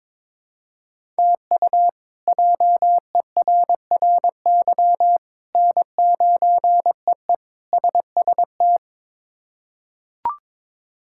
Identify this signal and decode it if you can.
Morse code, heard as a keyed tone, about 21 wpm, 700 Hz. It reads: TU JERRY N9EE SST
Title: K1USN SST QSO Practice - 22wpm